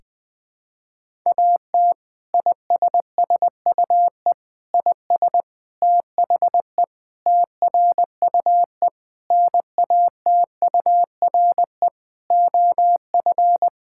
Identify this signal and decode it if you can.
Morse code, heard as a keyed tone, about 20 wpm, 700 Hz. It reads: AT ISSUE IS THE TRUE NATURE OF
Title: Sentences from Top 1000 Words - Review Part 6 - 20wpm